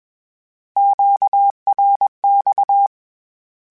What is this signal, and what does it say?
Morse code, heard as a keyed tone, about 21 wpm, 800 Hz. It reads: QRX
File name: Q Calls / Q Calls-mp3s / Q Calls Twice @20